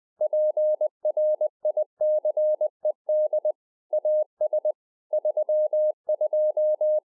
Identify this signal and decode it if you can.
Morse code, heard as a keyed tone, about 20 wpm, 610 Hz. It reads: PRICED AS 32
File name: 2011-04-mp3 / A SMALL TRANCEIVER 3 20WPM